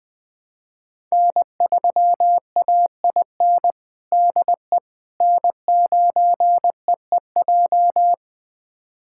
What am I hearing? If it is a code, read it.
Morse code, heard as a keyed tone, about 19 wpm, 700 Hz. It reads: N3AIN DE N9EEJ
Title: USA Callsign DE Callsign 20wpm